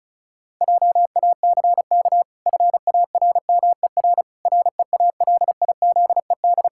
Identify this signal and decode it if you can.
Morse code, heard as a keyed tone, about 34 wpm, 700 Hz. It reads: JACK FARMER REALIZED
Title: Pangram Challenge v2 - 35wpm